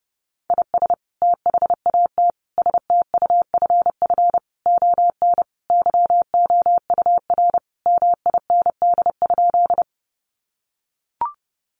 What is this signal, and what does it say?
Morse code, heard as a keyed tone, about 30 wpm, 700 Hz. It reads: IS THAT STUFF ON YOUR MIND?
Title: Sentences from Top 1400 Words 30wpm